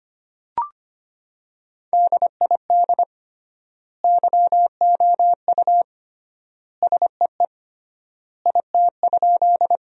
Morse code, heard as a keyed tone, about 25 wpm, 700 Hz. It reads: DID YOU SEE IT?
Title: 3x Word-Spacing - Sentences from Top 100 Words - 25wpm